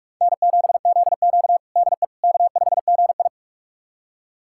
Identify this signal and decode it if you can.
Morse code, heard as a keyed tone, about 46 wpm, 700 Hz. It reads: N7ZQ DE K5GI
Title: USA Callsign DE Callsign - 4 Characters - 45wpm